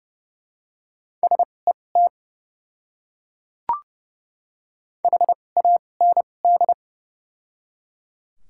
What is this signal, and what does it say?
Morse code, heard as a keyed tone, about 30 wpm, spaced as for 22 wpm, 700 Hz. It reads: SET HAND
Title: 2x Character Spacing - Top 200 Words - 30wpm